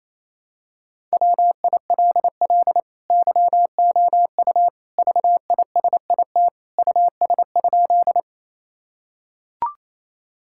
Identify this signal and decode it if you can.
Morse code, heard as a keyed tone, about 28 wpm, 700 Hz. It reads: WILL YOU VISIT US?
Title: Sentences from Top 1000 Words - Review Part 6 - 28wpm